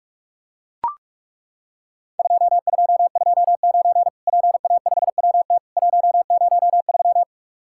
Morse code, heard as a keyed tone, about 41 wpm, 700 Hz. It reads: 2229 PA5WT 103
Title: Worked All Europe DX Contest - QTC Exchange - 45wpm